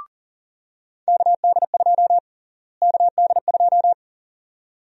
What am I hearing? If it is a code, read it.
Morse code, heard as a keyed tone, about 40 wpm, 700 Hz. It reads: KD2 KD2
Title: USA Call Sign Prefixes - Repeated 2x - 40wpm